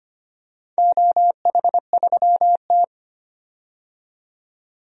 Morse code, heard as a keyed tone, about 25 wpm, 700 Hz. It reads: OH3T
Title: DXCC-100 Call Signs - 4 Characters - 25wpm